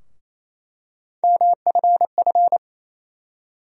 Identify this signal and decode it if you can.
Morse code, heard as a keyed tone, about 28 wpm, 700 Hz. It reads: MFF
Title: USA Call Sign Postfixes - 3 Letters 28wpm